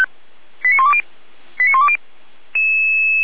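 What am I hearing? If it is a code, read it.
Five-tone selective call (ZVEI), heard as quick runs of five tones, tones of about 70 ms. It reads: 89120 89120
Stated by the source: ZVEI five-tone selective call 89340 received on an SDR.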